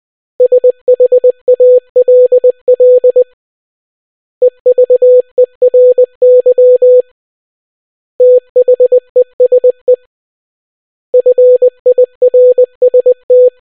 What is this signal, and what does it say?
Morse code, heard as a keyed tone, about 20 wpm, 500 Hz. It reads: SHALL EVERY THESE FIRST